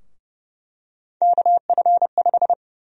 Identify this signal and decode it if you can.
Morse code, heard as a keyed tone, about 29 wpm, 700 Hz. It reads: KF5
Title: USA Call Sign Prefixes - 30wpm